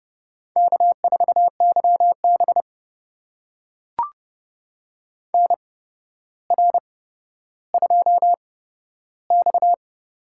Morse code, heard as a keyed tone, about 32 wpm, spaced as for 13 wpm, 700 Hz. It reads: K4YB NR2X
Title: USA Call Signs - 4 Characters - 8x Character Spacing - 30wpm